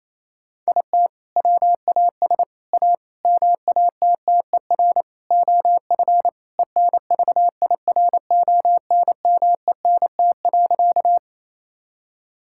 Morse code, heard as a keyed tone, about 27 wpm, 700 Hz. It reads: IT WAS A MATTER OF ENVIRONMENT.
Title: Sentences from Top 1000 Words - Review Part 5 - 28wpm